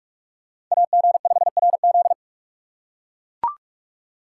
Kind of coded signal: Morse code

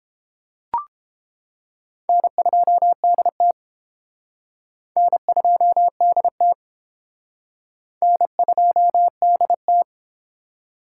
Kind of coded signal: Morse code

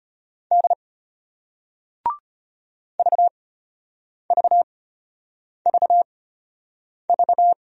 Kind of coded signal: Morse code

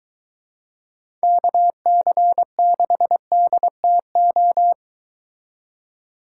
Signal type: Morse code

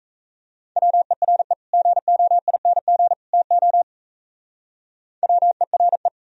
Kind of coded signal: Morse code

